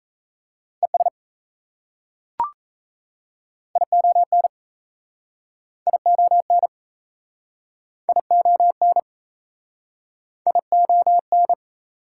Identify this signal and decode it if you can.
Morse code, heard as a keyed tone, about 35 wpm, 700 Hz. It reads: ES ION ION ION ION
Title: Speed-Racing Prefixes and Suffixes 28wpm